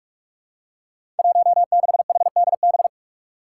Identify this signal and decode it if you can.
Morse code, heard as a keyed tone, about 44 wpm, 700 Hz. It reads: 16HDB
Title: Five Challenging Letter-Number Combinations 45wpm